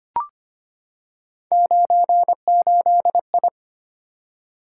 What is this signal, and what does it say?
Morse code, heard as a keyed tone, about 25 wpm, 700 Hz. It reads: 98I
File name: Three Letter-Number Combinations - 25wpm